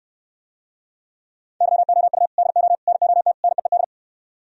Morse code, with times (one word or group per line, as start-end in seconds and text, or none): 1.6-2.3 YOU
2.4-2.8 DO
2.9-3.3 NOT
3.4-3.8 NEED